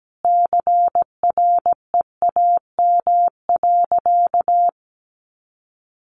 Morse code, heard as a keyed tone, about 17 wpm, 700 Hz. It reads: CREAM.
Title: Sentences from Top 1500 Words 17wpm